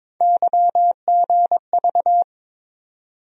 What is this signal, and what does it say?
Morse code, heard as a keyed tone, about 22 wpm, 700 Hz. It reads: YGV
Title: USA Call Sign Postfixes - 3 Letters 22wpm